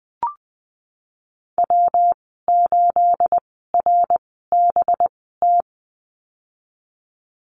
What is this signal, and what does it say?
Morse code, heard as a keyed tone, about 19 wpm, spaced as for 14 wpm, 700 Hz. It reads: W8RBT
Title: 219 Lesson 35g - Number 0 - Call Signs 20wpm